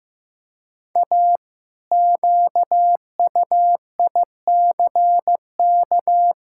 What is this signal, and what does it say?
Morse code, heard as a keyed tone, about 15 wpm, 700 Hz. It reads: A QUICK
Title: Pangram Challenge v2 - 15wpm